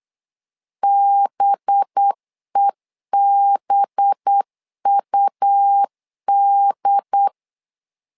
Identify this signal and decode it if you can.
Morse code, heard as a keyed tone, about 8 wpm, 800 Hz. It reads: BEBUD
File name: LEZIONI / 5 LBVC / 5 LB 8wpm 2816